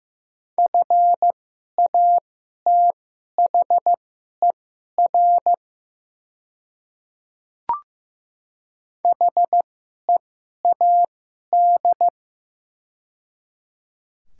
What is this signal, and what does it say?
Morse code, heard as a keyed tone, about 15 wpm, spaced as for 11 wpm, 700 Hz. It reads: FATHER HEAD
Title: 2x Character Spacing - Top 500 Words - 15wpm